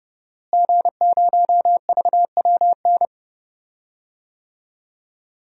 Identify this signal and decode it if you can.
Morse code, heard as a keyed tone, about 30 wpm, 700 Hz. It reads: G0VWN
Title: DXCC-100 Call Signs - 5 Characters - Repeated 2x - 30wpm